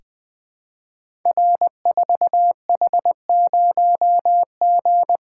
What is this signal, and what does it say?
Morse code, heard as a keyed tone, about 20 wpm, 700 Hz. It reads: R4H0G
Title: Code Groups - 5 Letter:Numbers - 20wpm